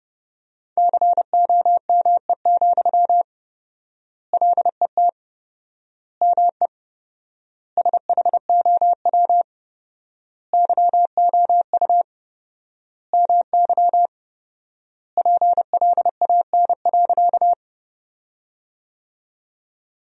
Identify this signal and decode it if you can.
Morse code, heard as a keyed tone, about 30 wpm, 700 Hz. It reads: COME, LET ME SHOW YOU MY PLAN.
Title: 4x Word-Spacing - Sentences from Top 500 Words - 30wpm